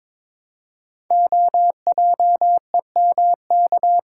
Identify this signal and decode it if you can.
Morse code, heard as a keyed tone, about 22 wpm, 700 Hz. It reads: OJEMK
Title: Code Groups - 5 Letters - 22wpm